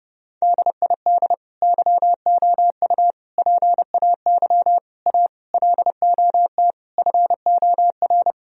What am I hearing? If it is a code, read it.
Morse code, heard as a keyed tone, about 30 wpm, 700 Hz. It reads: DID YOU PAY A LOT FOR
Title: Sentences from Top 1000 Words - Review Part 4 - 30wpm